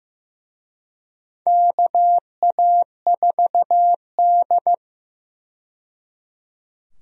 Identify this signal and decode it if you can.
Morse code, heard as a keyed tone, about 15 wpm, 700 Hz. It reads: KA4D
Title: USA Call Signs - 4 Characters - 15wpm